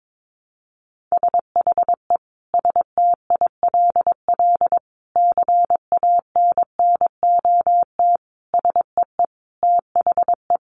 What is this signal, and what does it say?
Morse code, heard as a keyed tone, about 22 wpm, 700 Hz. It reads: SHE STILL CANNOT SEE THE
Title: Additional Easy Phrases 22wpm